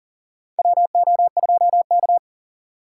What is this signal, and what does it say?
Morse code, heard as a keyed tone, about 39 wpm, 700 Hz. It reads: WO2K
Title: USA Call Signs - 4x Character Spacing - 40wpm